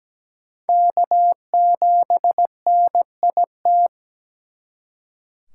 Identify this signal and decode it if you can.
Morse code, heard as a keyed tone, about 20 wpm, 700 Hz. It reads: K7NIT
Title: Speed Racing - USA 1x3 Call Signs 17wpm